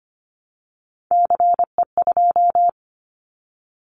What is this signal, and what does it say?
Morse code, heard as a keyed tone, about 25 wpm, 700 Hz. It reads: CE2
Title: Callsign Prefixes for Top 100 DXCC - 25wpm